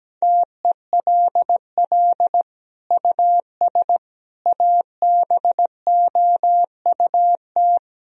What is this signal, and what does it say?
Morse code, heard as a keyed tone, about 17 wpm, 700 Hz. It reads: TELL US ABOUT
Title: Sentences from Top 1000 Words - Review Part 3 - 17wpm